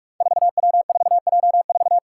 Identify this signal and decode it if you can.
Morse code, heard as a keyed tone, about 46 wpm, 700 Hz. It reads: 4W4J4
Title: Five Challenging Letter-Number Combinations 45wpm